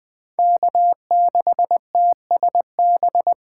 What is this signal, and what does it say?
Morse code, heard as a keyed tone, about 20 wpm, 700 Hz. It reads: K6TSB
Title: Speed Racing - USA 1x3 Call Signs 17wpm